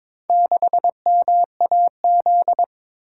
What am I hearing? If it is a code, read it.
Morse code, heard as a keyed tone, about 22 wpm, 700 Hz. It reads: 6MAZ
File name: practice / 22wpm / Four Letter-Number Combinations - 22wpm